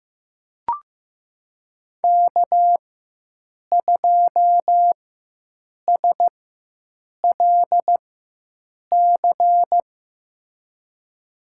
Morse code, heard as a keyed tone, about 15 wpm, spaced as for 7 wpm, 700 Hz. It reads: K2SLC
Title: USA Call Signs - 5 Characters - 4x Character Spacing - 15wpm